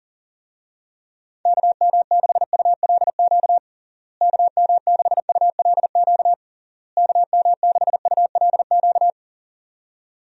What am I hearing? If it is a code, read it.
Morse code, heard as a keyed tone, about 40 wpm, 700 Hz. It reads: KM6ULQ KM6ULQ KM6ULQ
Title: USA Call Signs - 6 Characters - Repeated 3x - 40wpm